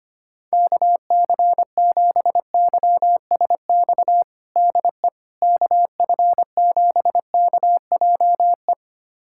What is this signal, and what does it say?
Morse code, heard as a keyed tone, about 25 wpm, 700 Hz. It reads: KC7YSX DE KF7KJE
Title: USA Callsign DE Callsign - 6 Characters - 25wpm